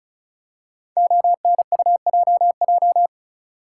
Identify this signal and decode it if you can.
Morse code, heard as a keyed tone, about 36 wpm, 700 Hz. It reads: ONUJJ
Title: Code Groups - 5 Letters - 35wpm